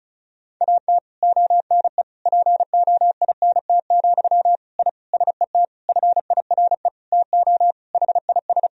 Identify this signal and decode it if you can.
Morse code, heard as a keyed tone, about 36 wpm, 700 Hz. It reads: AT ONE POINT, I SET FIRE TO HIS
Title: Sentences from Top 1000 Words - Review Part 1 - 35wpm